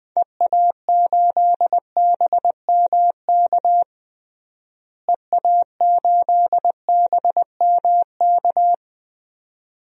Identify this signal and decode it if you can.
Morse code, heard as a keyed tone, about 20 wpm, 700 Hz. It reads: EA8BMK EA8BMK
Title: DXCC-100 Call Signs - 6 Characters - Repeated 2x - 20wpm